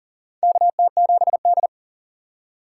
Morse code, heard as a keyed tone, about 40 wpm, 700 Hz. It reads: KT7D
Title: USA Call Signs - 4 Characters - 6x Character Spacing - 40wpm